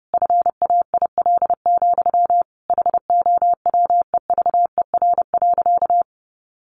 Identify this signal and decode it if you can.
Morse code, heard as a keyed tone, about 30 wpm, 700 Hz. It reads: FAIL, HOWEVER.
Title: Sentences from Top 1100 Words 30wpm